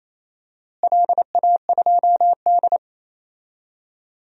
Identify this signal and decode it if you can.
Morse code, heard as a keyed tone, about 28 wpm, 700 Hz. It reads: LA2D
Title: DXCC-100 Call Signs - 4 Characters - Repeated 3x - 28wpm